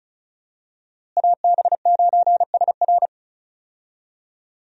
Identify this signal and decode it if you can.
Morse code, heard as a keyed tone, about 36 wpm, 700 Hz. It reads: AB9SR
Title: USA Call Signs - 5 Characters - Repeated 2x - 35wpm